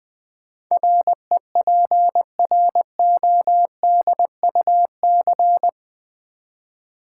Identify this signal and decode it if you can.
Morse code, heard as a keyed tone, about 21 wpm, 700 Hz. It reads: REPRODUC
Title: Build A Word - 4 Letters At A Time - 12-Letter Words - 20wpm